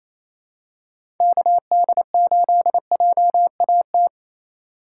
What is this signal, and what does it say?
Morse code, heard as a keyed tone, about 28 wpm, 700 Hz. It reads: KD8JAT
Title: USA Call Signs - 6 Characters - 3x Character Spacing - 28wpm